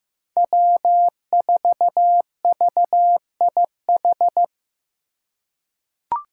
Morse code, heard as a keyed tone, about 15 wpm, 700 Hz. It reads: W4VIH
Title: Code Groups - 5 Letter:Numbers - 15wpm